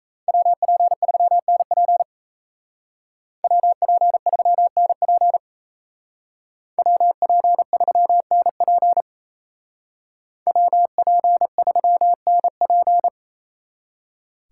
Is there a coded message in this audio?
Yes: Morse code from 0.3 to 13.1 s, heard as a keyed tone, about 36 wpm, 700 Hz. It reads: WP3NP WP3NP WP3NP WP3NP